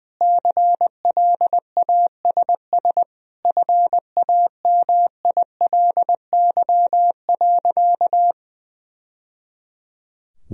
Morse code, heard as a keyed tone, about 21 wpm, 700 Hz. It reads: CLASS FAMILY.
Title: Sentences from Top 1000 Words - Review Part 2 - 20wpm